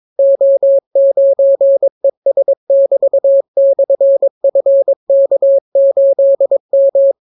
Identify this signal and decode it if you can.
Morse code, heard as a keyed tone, about 22 wpm, 550 Hz. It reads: O9ES<BT>/FK8M